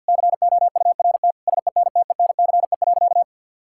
Morse code, heard as a keyed tone, about 50 wpm, 700 Hz. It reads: COURT SENTENCE.
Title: Sentences from Top 1600 Words 50wpm